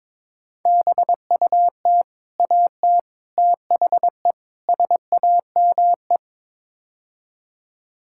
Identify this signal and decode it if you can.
Morse code, heard as a keyed tone, about 22 wpm, 700 Hz. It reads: BUT AT THE SAME
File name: practice / 22wpm / Build A Phrase - 5-Word Phrases - 22wpm